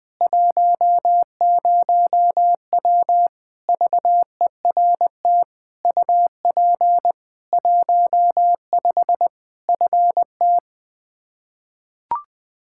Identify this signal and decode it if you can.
Morse code, heard as a keyed tone, about 20 wpm, 700 Hz. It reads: 10W VERT UP 15 FT
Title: Practice QSO - Exchange 2 - Info - 20wpm